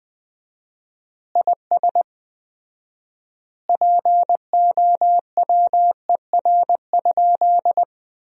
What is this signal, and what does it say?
Morse code, heard as a keyed tone, about 21 wpm, 700 Hz. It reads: IS POWER?
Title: 4x Word-Spacing - Sentences from Top 500 Words - 20wpm